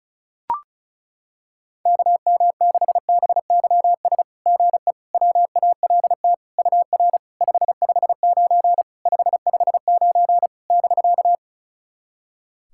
Morse code, heard as a keyed tone, about 34 wpm, 700 Hz. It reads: KM6BYS GE WALT UR 559 559 <BK>
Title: SOTA QSO Practice 35wpm